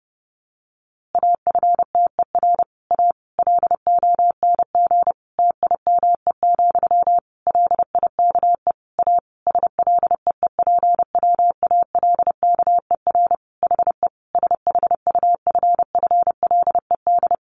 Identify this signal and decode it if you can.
Morse code, heard as a keyed tone, about 30 wpm, 700 Hz. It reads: AFTER A LONG TIME, LIKE A SLEEPWALKER HE SHUFFLED
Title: A Sound Of Thunder 30wpm